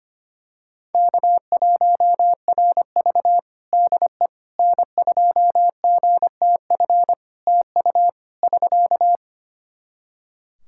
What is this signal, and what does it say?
Morse code, heard as a keyed tone, about 25 wpm, 700 Hz. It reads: K1RV DE N2GTF TU <SK>
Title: Practice QSO - Exchange 4 - Ending - 25wpm